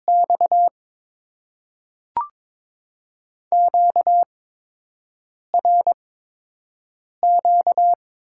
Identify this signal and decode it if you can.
Morse code, heard as a keyed tone, about 22 wpm, spaced as for 6 wpm, 700 Hz. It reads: X QRQ
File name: Top 90 QSO Elements - 8x Character Spacing - 22wpm